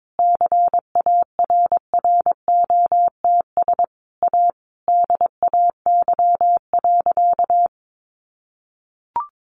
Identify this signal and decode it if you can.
Morse code, heard as a keyed tone, about 22 wpm, 700 Hz. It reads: CARROTS A DAY.
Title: Easy Phrases 22wpm